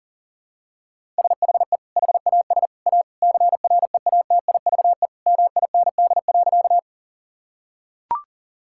Morse code, heard as a keyed tone, about 40 wpm, 700 Hz. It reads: SHE HAS A CREATIVE MIND.
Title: Sentences from Top 1000 Words - Review Part 5 - 40wpm